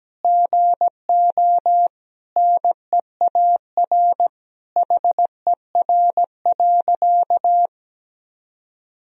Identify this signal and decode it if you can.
Morse code, heard as a keyed tone, about 17 wpm, 700 Hz. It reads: GO NEAR HER.
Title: Sentences from Top 1000 Words - Review Part 4 - 17wpm